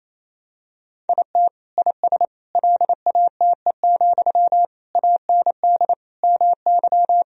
Morse code, heard as a keyed tone, about 28 wpm, 700 Hz. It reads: IT IS LATE, AND MY
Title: Sentences from Top 1000 Words - Review Part 3 - 28wpm